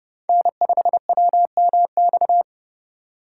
Morse code, heard as a keyed tone, about 30 wpm, 700 Hz. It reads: N5WMX